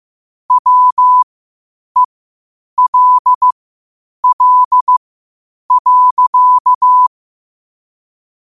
Morse code, heard as a keyed tone, about 15 wpm, spaced as for 8 wpm, 1000 Hz. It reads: WELL.
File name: qotdfiles / 8wpm